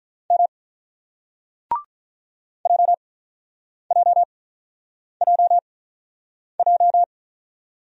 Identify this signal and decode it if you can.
Morse code, heard as a keyed tone, about 48 wpm, spaced as for 9 wpm, 700 Hz. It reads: M JJJJ